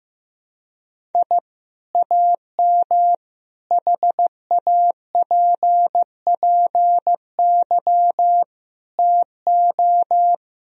Morse code, heard as a keyed tone, about 15 wpm, 700 Hz. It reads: I AM HAPPY TO
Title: Sentences from Top 1000 Words - Review Part 3 - 15wpm